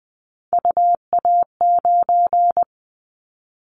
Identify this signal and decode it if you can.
Morse code, heard as a keyed tone, about 20 wpm, 700 Hz. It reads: UA9
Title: Callsign Prefixes for Top 50 DXCC - 20wpm